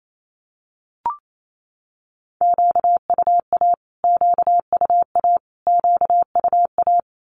Morse code, heard as a keyed tone, about 28 wpm, 700 Hz. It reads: QUA QUA QUA
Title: Q-Codes Repeated 3x 28wpm